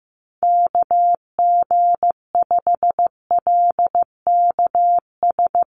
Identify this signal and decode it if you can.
Morse code, heard as a keyed tone, about 15 wpm, 700 Hz. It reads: KG5LKS
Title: USA Call Signs Repeated 3x 15wpm